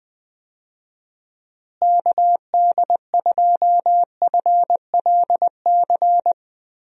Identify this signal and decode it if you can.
Morse code, heard as a keyed tone, about 20 wpm, 700 Hz. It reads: KD2FLC
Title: USA Call Signs - 6 Characters - Repeated 2x - 20wpm